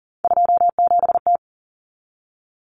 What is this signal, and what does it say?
Morse code, heard as a keyed tone, about 40 wpm, 700 Hz. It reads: 27T
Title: Common Cut Numbers - 3-Digit 40wpm